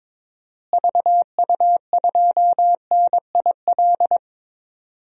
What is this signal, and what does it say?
Morse code, heard as a keyed tone, about 22 wpm, 700 Hz. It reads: VU2NIL
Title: DXCC-100 Call Signs - 6 Characters - Repeated 2x - 22wpm